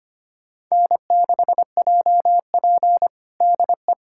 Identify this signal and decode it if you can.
Morse code, heard as a keyed tone, about 25 wpm, 700 Hz. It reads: N6JP DE
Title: USA Callsign DE Callsign - 4 Characters - 25wpm